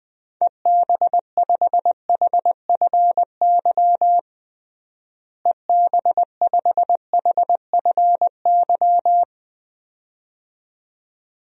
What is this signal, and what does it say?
Morse code, heard as a keyed tone, about 20 wpm, 700 Hz. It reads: EB5HFY EB5HFY
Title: DXCC-100 Call Signs - 6 Characters - Repeated 2x - 20wpm